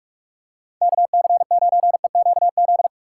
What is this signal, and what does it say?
Morse code, heard as a keyed tone, about 46 wpm, 700 Hz. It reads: KC9EQZ
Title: USA Call Signs - 6 Characters - 8x Character Spacing - 45wpm